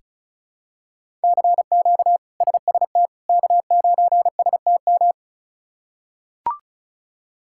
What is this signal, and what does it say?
Morse code, heard as a keyed tone, about 34 wpm, 700 Hz. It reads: CQ SST K9STM
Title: K1USN SST QSO Practice - 35wpm